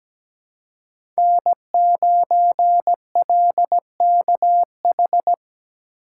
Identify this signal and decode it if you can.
Morse code, heard as a keyed tone, about 17 wpm, 700 Hz. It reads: N9LKH
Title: USA Call Signs - 5 Characters - 4x Character Spacing - 17wpm